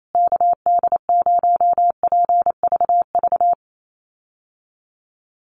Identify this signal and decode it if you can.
Morse code, heard as a keyed tone, about 28 wpm, 700 Hz. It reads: KD0PVV
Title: USA Call Signs Repeated 3x 28wpm